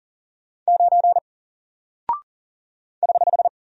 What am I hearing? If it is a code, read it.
Morse code, heard as a keyed tone, about 39 wpm, 700 Hz. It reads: 9 <HH>
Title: Single Character (Letter:Number:Punctuation:Prosign) 40wpm